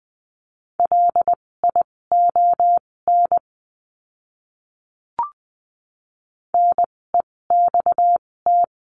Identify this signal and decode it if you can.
Morse code, heard as a keyed tone, about 20 wpm, spaced as for 16 wpm, 700 Hz. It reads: LION NEXT
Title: 256 Lesson 42a - Words 12-20wpm